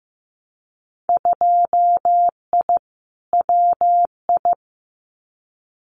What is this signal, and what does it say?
Morse code, heard as a keyed tone, about 15 wpm, 700 Hz. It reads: 2I WI
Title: Winter Field Day Exchanges 15wpm